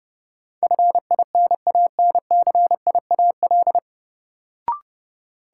Morse code, heard as a keyed tone, about 31 wpm, 700 Hz. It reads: FINANCIAL
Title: Sentences from Top 1000 Words - Review Part 5 - New Word Drill - 30wpm